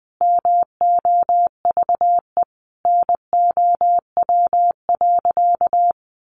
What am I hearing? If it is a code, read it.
Morse code, heard as a keyed tone, about 20 wpm, 700 Hz. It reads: MOVE NOW.
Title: Sentences from Top 1200 Words 20wpm